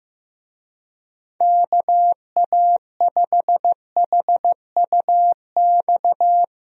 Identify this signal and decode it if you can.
Morse code, heard as a keyed tone, about 15 wpm, 700 Hz. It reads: KA5HUX
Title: USA Call Signs - 6 Characters - Repeated 2x - 15wpm